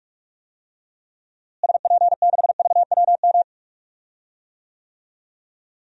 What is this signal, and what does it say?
Morse code, heard as a keyed tone, about 46 wpm, 700 Hz. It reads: SP6VWM
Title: DXCC-100 Call Signs - 6 Characters - Repeated 2x - 45wpm